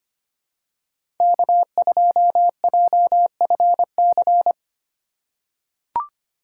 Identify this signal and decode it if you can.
Morse code, heard as a keyed tone, about 25 wpm, 700 Hz. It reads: K2JFC
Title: USA Call Signs - 5 Characters - 4x Character Spacing - 25wpm